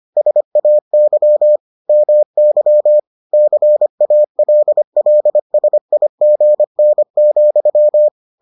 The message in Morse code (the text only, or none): SAY MY CALLSIGN,